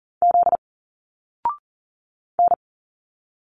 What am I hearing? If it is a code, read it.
Morse code, heard as a keyed tone, about 40 wpm, 700 Hz. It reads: Z N
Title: Single Letters 40wpm